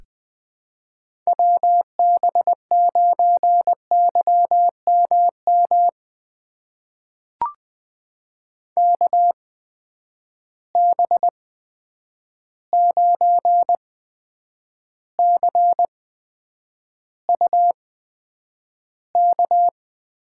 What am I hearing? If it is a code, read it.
Morse code, heard as a keyed tone, about 20 wpm, spaced as for 9 wpm, 700 Hz. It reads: WB9YMM KB9CUK
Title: USA Call Signs - 6 Characters - 8x Character Spacing - 20wpm